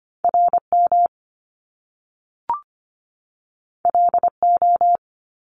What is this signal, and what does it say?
Morse code, heard as a keyed tone, about 25 wpm, 700 Hz. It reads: R M L O
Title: Top Two Letter Combinations in Words 25wpm